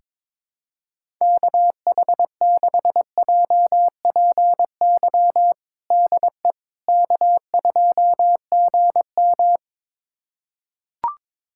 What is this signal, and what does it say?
Morse code, heard as a keyed tone, about 22 wpm, 700 Hz. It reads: KH6JPY DE K2GM
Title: USA Callsign DE Callsign 22wpm